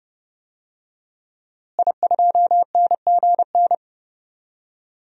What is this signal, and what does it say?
Morse code, heard as a keyed tone, about 30 wpm, 700 Hz. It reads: I2NGN
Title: DXCC-100 Call Signs - 5 Characters - Repeated 2x - 30wpm